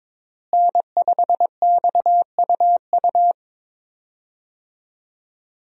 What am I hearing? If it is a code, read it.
Morse code, heard as a keyed tone, about 22 wpm, 700 Hz. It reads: N5XUU
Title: USA Call Signs - 5 Characters - Repeated 2x - 22wpm